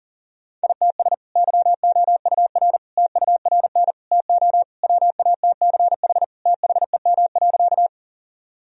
Morse code, heard as a keyed tone, about 40 wpm, 700 Hz. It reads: ITS YOUR TURN TO WATCH THEM.
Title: Sentences from Top 1000 Words - Review Part 1 - 40wpm